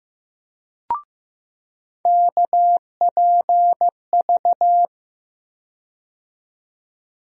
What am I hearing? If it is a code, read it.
Morse code, heard as a keyed tone, about 15 wpm, 700 Hz. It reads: KPV
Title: USA Call Sign Postfixes - 3 Letters 15wpm